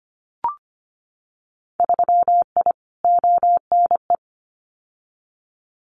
Morse code, heard as a keyed tone, about 25 wpm, 690 Hz. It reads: 3I ONE
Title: Winter Field Day Exchanges Encore 25wpm